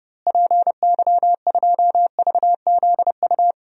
Morse code, heard as a keyed tone, about 31 wpm, 700 Hz. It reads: PY2VZU
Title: DXCC-100 Call Signs - 6 Characters - Repeated 3x - 30wpm